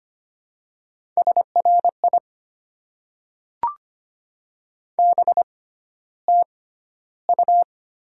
Morse code, heard as a keyed tone, about 25 wpm, spaced as for 13 wpm, 700 Hz. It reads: SRI BTU
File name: Top 90 QSO Elements - 6x Character Spacing - 25wpm